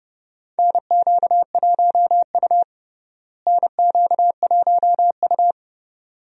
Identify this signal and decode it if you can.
Morse code, heard as a keyed tone, about 32 wpm, 700 Hz. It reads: NQ1U NQ1U
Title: USA Call Signs - 4 Characters - Repeated 2x - 30wpm